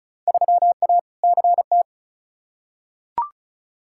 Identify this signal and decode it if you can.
Morse code, heard as a keyed tone, about 34 wpm, 700 Hz. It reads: 3A CT
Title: Field Day Exchanges - 35wpm